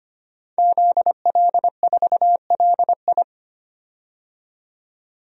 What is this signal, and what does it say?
Morse code, heard as a keyed tone, about 25 wpm, 700 Hz. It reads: ZL4LI
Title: DXCC-100 Call Signs - 5 Characters - 25wpm